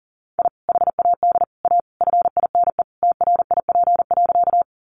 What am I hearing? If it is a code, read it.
Morse code, heard as a keyed tone, about 40 wpm, 700 Hz. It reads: I HAD A FINE TRIP.
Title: Sentences from Top 1300 Words 40wpm